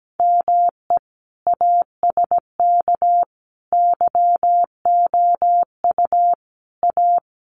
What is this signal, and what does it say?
Morse code, heard as a keyed tone, about 17 wpm, 700 Hz. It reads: ME ASK YOU A
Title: Sets of 5 Words Encore2 17wpm